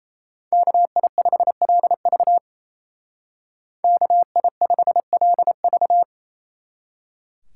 Morse code, heard as a keyed tone, about 36 wpm, 700 Hz. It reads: KI5LV KI5LV